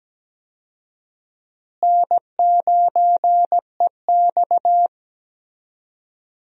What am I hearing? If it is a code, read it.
Morse code, heard as a keyed tone, about 17 wpm, 700 Hz. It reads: N9EX
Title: USA Call Signs - 4 Characters - Repeated 3x - 17wpm